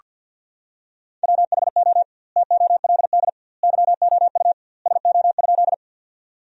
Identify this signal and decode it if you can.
Morse code, heard as a keyed tone, about 48 wpm, 700 Hz. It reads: WHO TOLD YOU SO?
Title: 2x Word-Spacing - Sentences from Top 400 Words - 50wpm